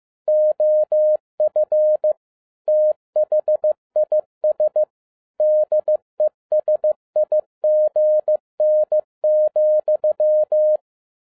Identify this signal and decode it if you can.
Morse code, heard as a keyed tone, about 15 wpm, 610 Hz. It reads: OF THIS DESIGN,
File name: 2012-01-mp3 / n13-A NEWLY DESIGNED QSL CARD-15wpm